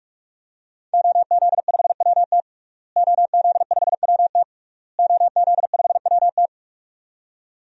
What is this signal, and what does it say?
Morse code, heard as a keyed tone, about 46 wpm, 700 Hz. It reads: OZ5WT OZ5WT OZ5WT
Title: DXCC-100 Call Signs - 5 Characters - Repeated 3x - 45wpm